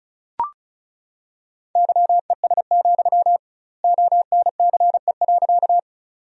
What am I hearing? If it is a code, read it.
Morse code, heard as a keyed tone, about 36 wpm, 700 Hz. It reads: YES, ONCE.
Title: 2x Word-Spacing - Sentences from Top 500 Words - 35wpm